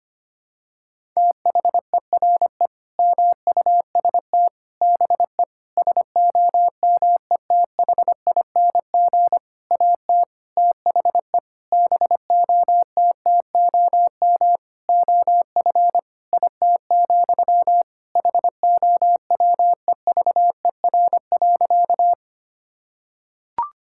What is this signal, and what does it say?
Morse code, heard as a keyed tone, about 25 wpm, 700 Hz. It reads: THERE MUST BE SOMETHING AT THE BOTTOM OF IT, HOWEVER.
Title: Sentences from Top 1000 Words - Review Part 6 - 25wpm